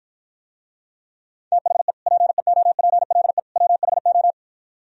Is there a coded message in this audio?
Yes: Morse code from 1.5 to 4.3 s, heard as a keyed tone, about 52 wpm, 700 Hz. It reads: THE PEOPLE WHO